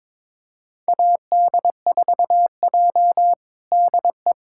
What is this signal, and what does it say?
Morse code, heard as a keyed tone, about 21 wpm, 700 Hz. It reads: AD4J DE
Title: USA Callsign DE Callsign - 4 Characters - 22wpm